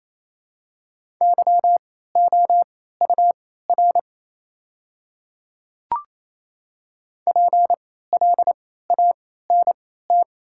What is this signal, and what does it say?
Morse code, heard as a keyed tone, about 28 wpm, spaced as for 16 wpm, 700 Hz. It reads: YOUR PLANT
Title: 3x Character Spacing - Top 500 Words - 28wpm